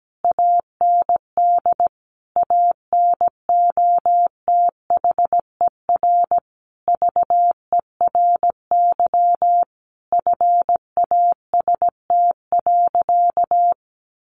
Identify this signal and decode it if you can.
Morse code, heard as a keyed tone, about 17 wpm, 700 Hz. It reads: AND ANOTHER VERY FAST.
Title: Sentences from Top 1100 Words 17wpm